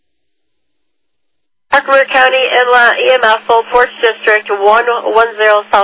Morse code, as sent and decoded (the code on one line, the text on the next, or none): none